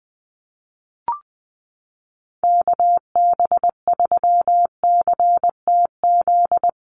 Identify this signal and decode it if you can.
Morse code, heard as a keyed tone, about 20 wpm, 700 Hz. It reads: KB3CTZ
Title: USA Call Signs Repeated 3x 20wpm